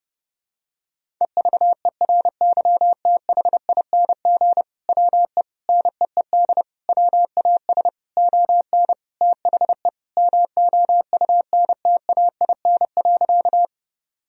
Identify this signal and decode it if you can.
Morse code, heard as a keyed tone, about 31 wpm, 700 Hz. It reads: EVERYTHING WE NEED WAS ON THE MOUNTAIN.
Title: Sentences from Top 1000 Words - Review Part 1 - 30wpm